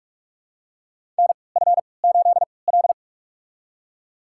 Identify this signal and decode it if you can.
Morse code, heard as a keyed tone, about 47 wpm, spaced as for 26 wpm, 700 Hz. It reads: NF8L
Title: USA Call Signs - 3x Character Spacing - 45wpm